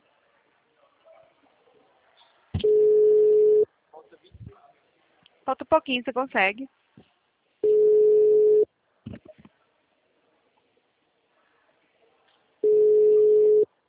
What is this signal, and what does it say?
Ringback tone (Europe, 425 Hz), with speech.